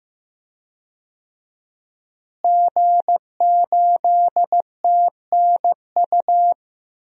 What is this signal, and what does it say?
Morse code, heard as a keyed tone, about 15 wpm, 700 Hz. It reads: G8TNU
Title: DXCC-100 Call Signs - 5 Characters - Repeated 3x - 15wpm